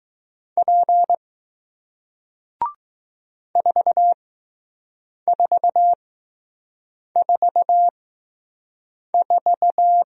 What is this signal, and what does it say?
Morse code, heard as a keyed tone, about 20 wpm, spaced as for 6 wpm, 700 Hz. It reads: P 4444